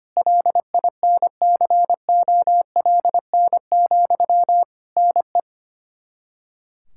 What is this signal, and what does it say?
Morse code, heard as a keyed tone, about 25 wpm, 700 Hz. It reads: LINCOLN, NE
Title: Practice QSO - Exchange 1 - Beginning - 25wpm